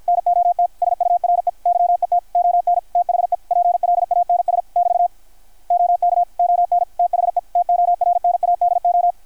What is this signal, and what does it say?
Morse code, heard as a keyed tone, about 50 wpm, 700 Hz. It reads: NOT SURE YET ON THE PLANS <BT> OK ON THE TORNADO
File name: archived / CRSnet